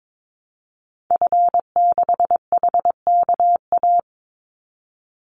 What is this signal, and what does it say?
Morse code, heard as a keyed tone, about 22 wpm, 700 Hz. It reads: F6HKA
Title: Call Signs Repeated 2x - 22wpm